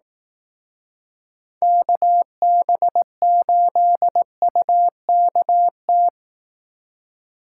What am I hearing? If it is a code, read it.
Morse code, heard as a keyed tone, about 18 wpm, 700 Hz. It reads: KB8UKT
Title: Speed Racing - USA 2x3 Call Signs 15wpm